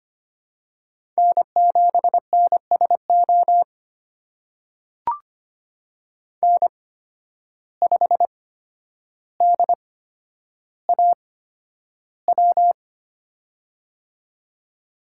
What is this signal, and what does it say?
Morse code, heard as a keyed tone, about 25 wpm, spaced as for 11 wpm, 700 Hz. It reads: N7NSO N5DAW
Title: USA Call Signs - 5 Characters - 8x Character Spacing - 25wpm